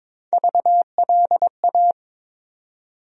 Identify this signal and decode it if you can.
Morse code, heard as a keyed tone, about 22 wpm, 700 Hz. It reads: VLA
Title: USA Call Sign Postfixes - 3 Letters 22wpm